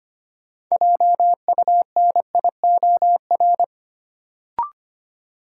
Morse code, heard as a keyed tone, about 25 wpm, 700 Hz. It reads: JUNIOR
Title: Sentences from Top 1000 Words - Review Part 7 - New Word Drill - 25wpm